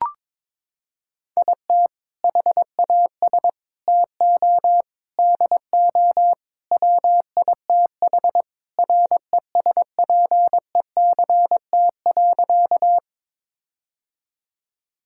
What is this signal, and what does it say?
Morse code, heard as a keyed tone, about 22 wpm, 700 Hz. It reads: IT HAS TO DO WITH RESPECT.
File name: practice / 22wpm / Sentences with 4-Word Phrases - 22wpm